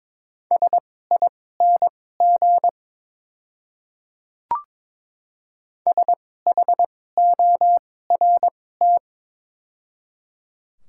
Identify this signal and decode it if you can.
Morse code, heard as a keyed tone, about 23 wpm, spaced as for 16 wpm, 700 Hz. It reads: SING SHORT
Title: 2x Character Spacing - Top 500 Words - 22wpm